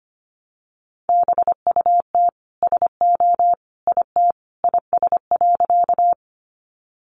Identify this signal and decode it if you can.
Morse code, heard as a keyed tone, about 25 wpm, 700 Hz. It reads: BUT SO IT IS.
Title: Sentences from Top 100 Words 25wpm